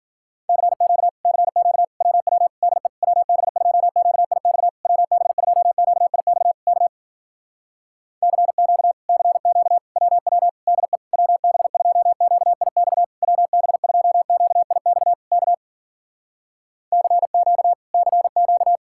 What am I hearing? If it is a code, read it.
Morse code, heard as a keyed tone, about 52 wpm, 700 Hz. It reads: CQ CQ WW DE WB2QIX WB2QIX K CQ CQ WW DE WB2QIX WB2QIX K CQ CQ